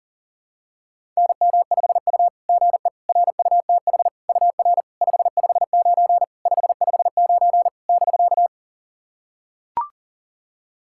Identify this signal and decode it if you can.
Morse code, heard as a keyed tone, about 40 wpm, 690 Hz. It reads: NM5U GE RUTH UR 559 559 <BK>
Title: SOTA QSO Practice 40wpm